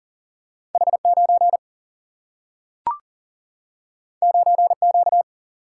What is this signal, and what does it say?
Morse code, heard as a keyed tone, about 40 wpm, 700 Hz. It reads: H9 9Q
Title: Single-Letter Single-Number 40wpm